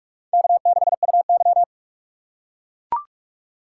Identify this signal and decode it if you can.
Morse code, heard as a keyed tone, about 42 wpm, 700 Hz. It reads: K6UY
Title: USA Call Signs - 4x Character Spacing - 45wpm